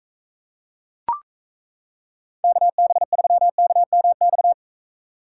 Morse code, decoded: KB3KMX

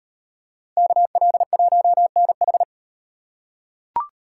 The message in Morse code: KL1NH

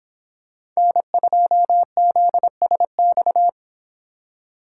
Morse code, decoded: N2ZSX

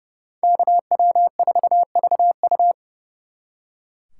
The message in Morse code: KW4VU

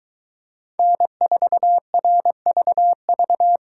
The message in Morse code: N4RVV

